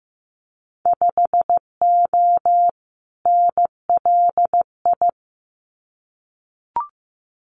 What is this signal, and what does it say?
Morse code, heard as a keyed tone, about 15 wpm, 700 Hz. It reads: 5O NLI
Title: Winter Field Day Exchanges 15wpm